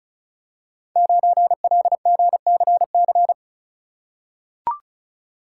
Morse code, heard as a keyed tone, about 33 wpm, 700 Hz. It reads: 9LGCC
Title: Code Groups - 5 Letter:Numbers - 35wpm